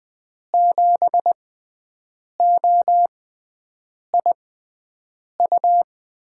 Morse code, heard as a keyed tone, about 21 wpm, spaced as for 7 wpm, 700 Hz. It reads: 7OIU
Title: USA Call Signs - 6 Characters - 6x Character Spacing - 20wpm